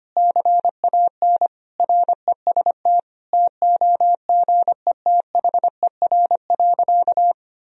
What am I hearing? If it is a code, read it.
Morse code, heard as a keyed tone, about 25 wpm, 700 Hz. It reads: CAN REST TOGETHER.
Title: Sentences from Top 500 Words and Abbreviations 25wpm